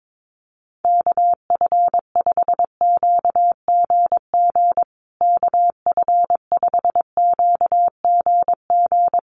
Morse code, heard as a keyed tone, about 22 wpm, 700 Hz. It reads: KF5QGG KF5QGG
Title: USA Call Signs Repeated 2x 22wpm